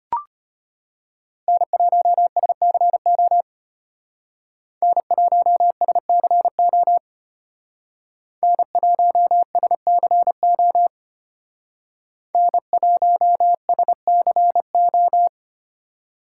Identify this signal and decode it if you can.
Morse code, heard as a keyed tone, about 32 wpm, 700 Hz. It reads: N1SCO N1SCO N1SCO N1SCO